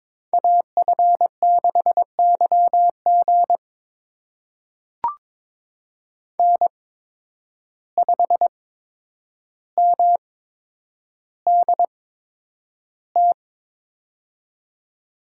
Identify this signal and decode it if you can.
Morse code, heard as a keyed tone, about 22 wpm, spaced as for 9 wpm, 700 Hz. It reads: AF6YG N5MDT
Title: USA Call Signs - 5 Characters - 8x Character Spacing - 22wpm